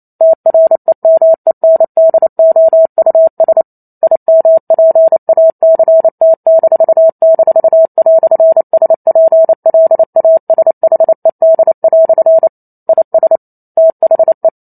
Morse code, heard as a keyed tone, about 28 wpm, 650 Hz. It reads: TREMENDOUS IMPACT--"SPLASHED" IS THE